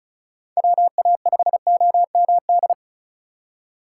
Morse code, heard as a keyed tone, about 36 wpm, 700 Hz. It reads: WA5OMD
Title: USA Call Signs - 6 Characters - 8x Character Spacing - 35wpm